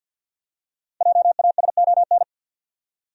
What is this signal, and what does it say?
Morse code, heard as a keyed tone, about 46 wpm, 700 Hz. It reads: JASON
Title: USA Common Names - Max 5 Letters - 50wpm